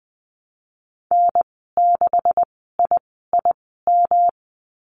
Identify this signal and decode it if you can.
Morse code, heard as a keyed tone, about 20 wpm, spaced as for 15 wpm, 700 Hz. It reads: N6IIM
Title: 154 Lesson 25g - Letter p - Call Signs 20wpm